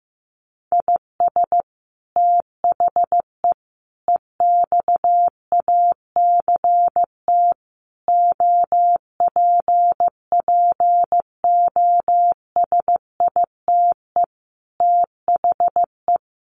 Morse code, heard as a keyed tone, about 15 wpm, 700 Hz. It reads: IS THE EXACT OPPOSITE THE
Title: Sentences from Top 1300 Words 15wpm